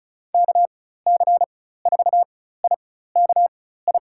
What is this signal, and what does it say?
Morse code, heard as a keyed tone, about 36 wpm, spaced as for 16 wpm, 700 Hz. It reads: KC4IKI
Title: USA Call Signs - 6 Characters - 4x Character Spacing - 35wpm